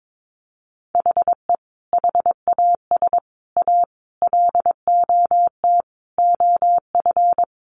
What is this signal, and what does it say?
Morse code, heard as a keyed tone, about 22 wpm, 700 Hz. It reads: HE HAS A LOT OF
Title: Additional Easy Phrases 22wpm